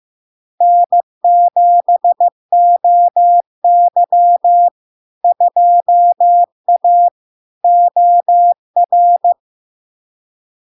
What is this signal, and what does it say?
Morse code, heard as a keyed tone, about 15 wpm, 700 Hz. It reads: N7OY 2A OR
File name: LARK-FD-2022-Log-in-CW-15wpm-700-Hz